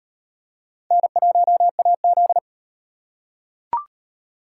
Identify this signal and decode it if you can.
Morse code, heard as a keyed tone, about 33 wpm, 700 Hz. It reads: N1AZ